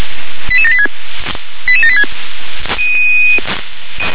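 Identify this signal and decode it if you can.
Five-tone selective call (ZVEI), heard as quick runs of five tones, tones of about 70 ms. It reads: 88786 88786
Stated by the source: ZVEI five-tone selective call received on an SDR.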